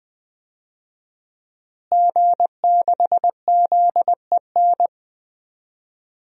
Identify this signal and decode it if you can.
Morse code, heard as a keyed tone, about 20 wpm, 700 Hz. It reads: G6ZEN
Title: DXCC-100 Call Signs - 5 Characters - Repeated 3x - 20wpm